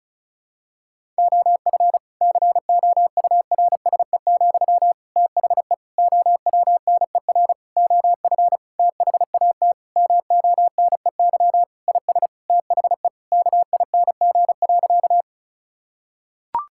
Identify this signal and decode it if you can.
Morse code, heard as a keyed tone, about 36 wpm, 700 Hz. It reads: OF COURSE, THE OWNER OF THAT MONEY IS THE KING.
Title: Sentences from Top 1600 Words 35wpm